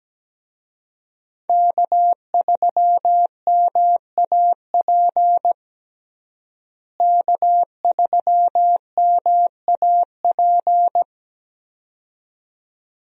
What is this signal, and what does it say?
Morse code, heard as a keyed tone, about 17 wpm, 700 Hz. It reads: K3MAP K3MAP
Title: USA Call Signs - 5 Characters - Repeated 2x - 17wpm